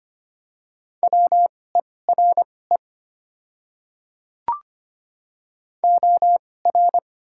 Morse code, heard as a keyed tone, about 25 wpm, spaced as for 18 wpm, 690 Hz. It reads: WERE OR
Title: 2x Character Spacing - Top 200 Words - 25wpm